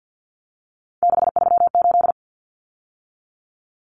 Morse code, heard as a keyed tone, about 50 wpm, 700 Hz. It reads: 638
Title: Numbers - 3-Digit 50wpm